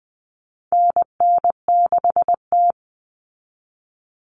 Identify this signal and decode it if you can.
Morse code, heard as a keyed tone, about 20 wpm, 700 Hz. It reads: NN6T
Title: Call Signs - 20wpm